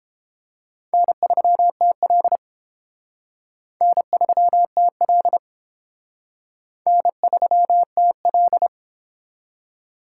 Morse code, heard as a keyed tone, about 31 wpm, 700 Hz. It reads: N3TL N3TL N3TL